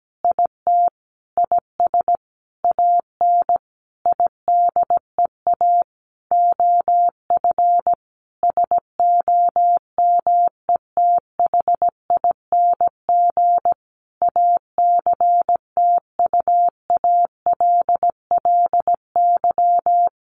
Morse code, heard as a keyed tone, about 17 wpm, 700 Hz. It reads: IT IS AN IDEA OF SOMETHING ACTUALLY
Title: Sentences from Top 1500 Words 17wpm